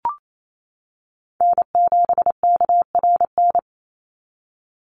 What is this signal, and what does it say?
Morse code, heard as a keyed tone, about 28 wpm, 700 Hz. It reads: N7KRN
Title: NA SOTA Chaser Callsigns 28wpm